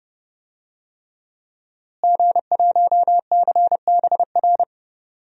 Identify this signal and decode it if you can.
Morse code, heard as a keyed tone, about 30 wpm, 700 Hz. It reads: G1CBR
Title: DXCC-100 Call Signs - 5 Characters - Repeated 3x - 30wpm